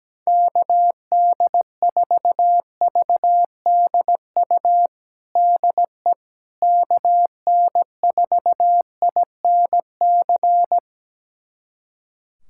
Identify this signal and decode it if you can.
Morse code, heard as a keyed tone, about 17 wpm, 700 Hz. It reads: KD4VDU DE KN4INC
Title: USA Callsign DE Callsign - 6 Characters - 17wpm